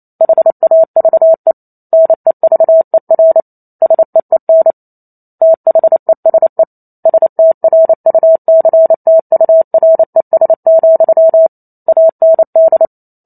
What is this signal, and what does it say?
Morse code, heard as a keyed tone, about 28 wpm, 650 Hz. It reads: HAVE NEVER SEEN THESE STRUCTURES, AND